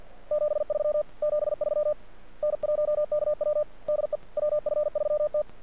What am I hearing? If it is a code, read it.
Morse code, heard as a keyed tone, about 50 wpm, 600 Hz. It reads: 73 73 N1KW DE WF3T
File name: archived / CRSnet